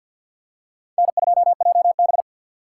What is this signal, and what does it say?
Morse code, heard as a keyed tone, about 53 wpm, 700 Hz. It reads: N2JB
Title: USA Call Signs - 4 Characters - 2x Character Spacing - 50wpm